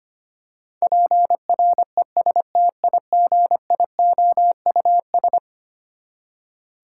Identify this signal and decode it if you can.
Morse code, heard as a keyed tone, about 25 wpm, 700 Hz. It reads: PRESTIGIOUS
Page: morse-code-ninja-practice-25wpm